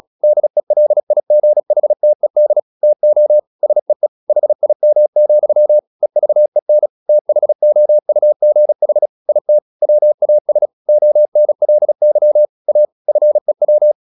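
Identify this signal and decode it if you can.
Morse code, heard as a keyed tone, about 37 wpm, 600 Hz. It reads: DELIGHTED TO SEE HIM, EVEN THOUGH IT WAS ONLY A FEW
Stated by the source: Morse practice at 36 wpm, text of Ken Follett's novel Never, from CW med Gnister.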